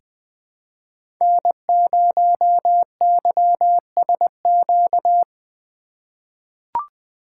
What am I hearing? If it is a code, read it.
Morse code, heard as a keyed tone, about 20 wpm, 700 Hz. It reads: N0YSQ
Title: K1USN SST QSO Practice - 20wpm